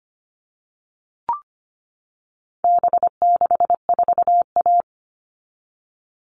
Morse code, heard as a keyed tone, about 25 wpm, 700 Hz. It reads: B64A
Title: All Cut Numbers - 4-Digit 25wpm